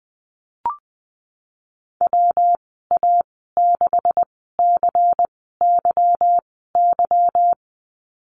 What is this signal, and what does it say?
Morse code, heard as a keyed tone, about 20 wpm, spaced as for 15 wpm, 700 Hz. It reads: WA6CYY
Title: 193 Lesson 31g - Letter v - Call Signs 20wpm